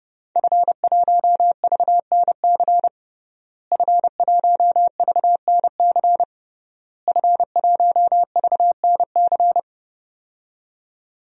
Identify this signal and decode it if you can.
Morse code, heard as a keyed tone, about 30 wpm, 700 Hz. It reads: F1VNC F1VNC F1VNC
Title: DXCC-100 Call Signs - 5 Characters - Repeated 3x - 30wpm